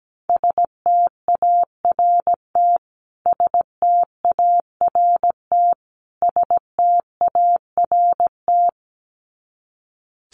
Morse code, heard as a keyed tone, about 17 wpm, 700 Hz. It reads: START START START
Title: Top 500 Words Repeated 17wpm